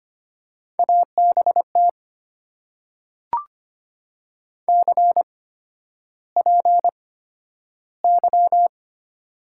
Morse code, heard as a keyed tone, about 25 wpm, spaced as for 11 wpm, 700 Hz. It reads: ABT CPY
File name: Top 90 QSO Elements - 8x Character Spacing - 25wpm